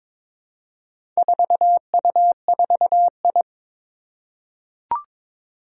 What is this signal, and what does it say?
Morse code, heard as a keyed tone, about 22 wpm, 700 Hz. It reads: 4U4I
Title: Four Challenging Letter-Number Combinations 22wpm